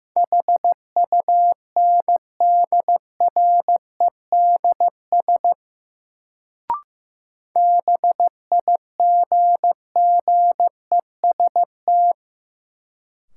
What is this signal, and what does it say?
Morse code, heard as a keyed tone, about 15 wpm, 700 Hz. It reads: HUNDREDS BIGGEST
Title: Sets of 4 Words - New Word Drill - 15wpm